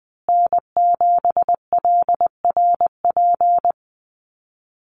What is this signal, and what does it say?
Morse code, heard as a keyed tone, about 20 wpm, 700 Hz. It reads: N7LRP
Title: USA Call Signs Repeated 2x 20wpm